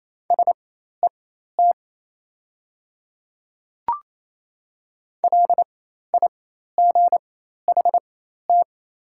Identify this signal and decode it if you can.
Morse code, heard as a keyed tone, about 28 wpm, spaced as for 13 wpm, 700 Hz. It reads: SET LIGHT
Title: 4x Character Spacing - Top 300 Words - 28wpm